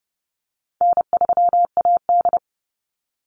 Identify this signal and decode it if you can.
Morse code, heard as a keyed tone, about 30 wpm, 700 Hz. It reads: N3AD
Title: Easy Call Signs - 30wpm